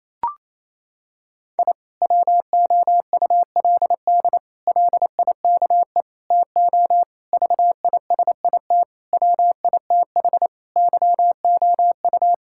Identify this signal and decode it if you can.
Morse code, heard as a keyed tone, about 28 wpm, 700 Hz. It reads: I WOULD LIKE TO VISIT WITH YOU
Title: Sentences from Top 1000 Words - Review Part 6 - 28wpm